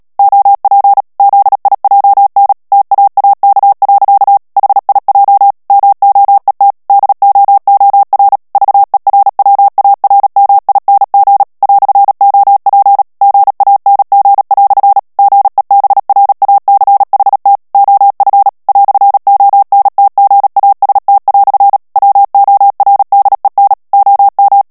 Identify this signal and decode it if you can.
Morse code, heard as a keyed tone, about 37 wpm, 800 Hz. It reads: OP ZIJN TAAK. HIJ MOET DOOR VERWARMING "OP GANG" GEBRACHT OF "ONTGAST" WORDEN OM
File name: morse mp3 / HOLLANDAIS texte clair 2 vitesse 35 mots minute